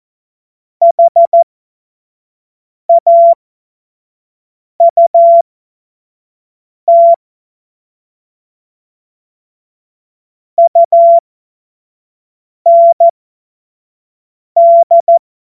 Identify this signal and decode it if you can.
Morse code, heard as a keyed tone, about 14 wpm, spaced as for 5 wpm, 680 Hz. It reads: HAUT UND
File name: Morsekurs / Witze / Witz 24. August 2021, 25 F70